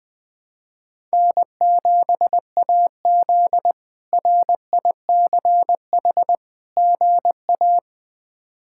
Morse code, heard as a keyed tone, about 21 wpm, 700 Hz. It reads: N7AZ RICH GA
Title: K1USN SST QSO Practice - 20wpm